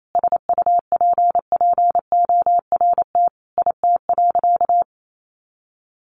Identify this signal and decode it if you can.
Morse code, heard as a keyed tone, about 28 wpm, 700 Hz. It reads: SUPPORT IT.
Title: Sentences from Top 1300 Words 28wpm